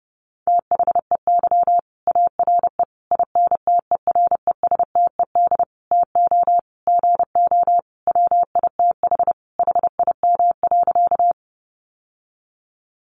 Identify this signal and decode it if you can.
Morse code, heard as a keyed tone, about 30 wpm, 700 Hz. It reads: THEY ARE INTERESTED TO GO WITH HIM.
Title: Sentences from Top 900 Words 30wpm